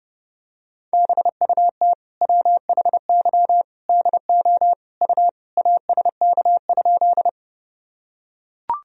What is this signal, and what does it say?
Morse code, heard as a keyed tone, about 30 wpm, 700 Hz. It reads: BUT WHY DO U ASK?
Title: Sentences from Top 200 Words and Abbreviations 30wpm